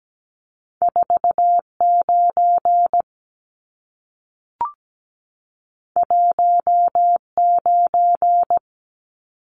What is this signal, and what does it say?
Morse code, heard as a keyed tone, about 17 wpm, 700 Hz. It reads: 49 19
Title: Numbers - 2-Digit 17wpm